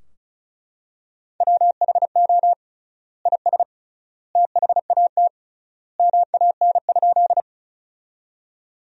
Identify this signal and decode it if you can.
Morse code, heard as a keyed tone, about 34 wpm, 700 Hz. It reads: WHO IS THAT MAN?
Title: 3x Word-Spacing - Sentences from Top 200 Words - 35wpm